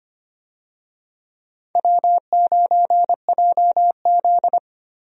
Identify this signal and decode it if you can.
Morse code, heard as a keyed tone, about 25 wpm, 700 Hz. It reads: W9JZ
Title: USA Call Signs - 4 Characters - Repeated 2x - 25wpm